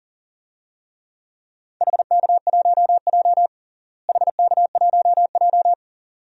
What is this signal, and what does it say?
Morse code, heard as a keyed tone, about 40 wpm, 700 Hz. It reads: HK1J HK1J
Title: DXCC-100 Call Signs - 4 Characters - Repeated 2x - 40wpm